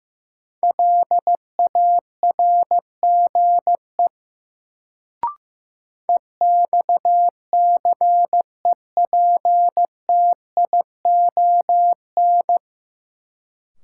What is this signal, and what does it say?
Morse code, heard as a keyed tone, about 15 wpm, 700 Hz. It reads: LARGE EXCEPTION
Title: Sets of 4 Words - New Word Drill - 15wpm